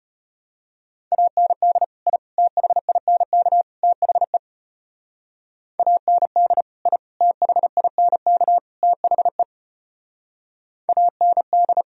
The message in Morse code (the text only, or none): AND I THINK THE AND I THINK THE AND